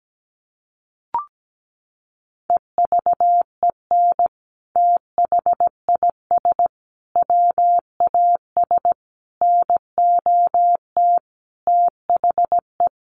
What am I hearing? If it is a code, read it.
Morse code, heard as a keyed tone, about 17 wpm, 700 Hz. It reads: EVEN THIS WAS NOT THE
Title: Sentences from Top 1100 Words 17wpm